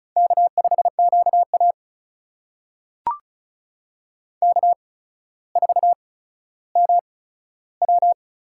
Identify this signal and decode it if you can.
Morse code, heard as a keyed tone, about 36 wpm, spaced as for 15 wpm, 700 Hz. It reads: K5QA K4MW
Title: USA Call Signs - 4 Characters - 8x Character Spacing - 35wpm